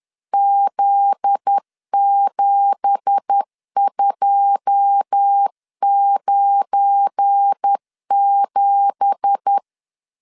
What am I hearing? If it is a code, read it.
Morse code, heard as a keyed tone, about 10 wpm, 800 Hz. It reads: Z7297
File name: CW / DET / 1 9 1500C S B 10wpm 4439